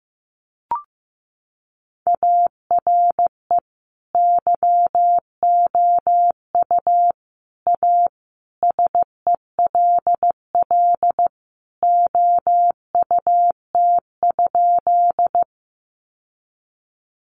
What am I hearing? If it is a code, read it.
Morse code, heard as a keyed tone, about 15 wpm, 700 Hz. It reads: ARE YOU A SELL OUT?
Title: Sentences from Top 1100 Words 15wpm